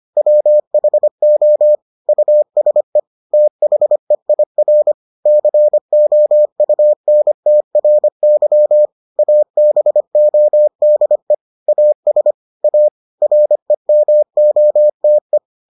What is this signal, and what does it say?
Morse code, heard as a keyed tone, about 25 wpm, 600 Hz. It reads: WHO USE THEIR COUNTRY ABODE AS A REMOTE